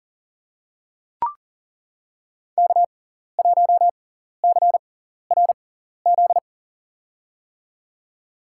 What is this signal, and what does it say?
Morse code, heard as a keyed tone, about 40 wpm, spaced as for 14 wpm, 700 Hz. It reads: K1CRZ
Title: USA Call Signs - 5 Characters - 6x Character Spacing - 40wpm